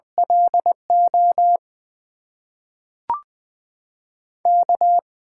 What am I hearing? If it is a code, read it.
Morse code, heard as a keyed tone, about 20 wpm, 700 Hz. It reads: LO K
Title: USA Call Signs - 6 Characters - 8x Character Spacing - 20wpm